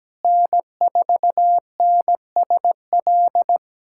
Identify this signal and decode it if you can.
Morse code, heard as a keyed tone, about 17 wpm, 700 Hz. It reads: N4NSL
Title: USA Callsign DE Callsign - 5 Characters - 17wpm